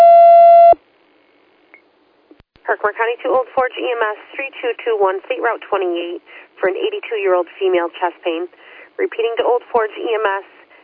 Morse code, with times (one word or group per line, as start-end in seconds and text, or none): none